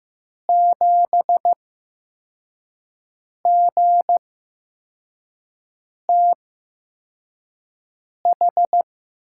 Morse code, heard as a keyed tone, about 15 wpm, spaced as for 4 wpm, 700 Hz. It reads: 7GTH
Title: USA Call Signs - 6 Characters - 8x Character Spacing - 15wpm